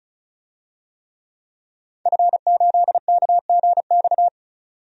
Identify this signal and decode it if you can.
Morse code, heard as a keyed tone, about 36 wpm, 700 Hz. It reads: F8KGX
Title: DXCC-100 Call Signs - 5 Characters - 35wpm